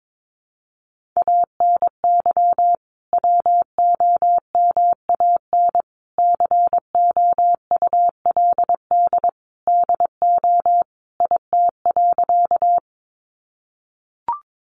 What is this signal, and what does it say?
Morse code, heard as a keyed tone, about 22 wpm, 700 Hz. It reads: ANY WOMAN COULD DO IT.
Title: Sentences from Top 1100 Words 22wpm